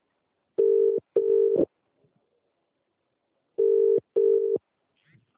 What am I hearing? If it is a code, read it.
Ringback tone (UK), 400 + 450 Hz.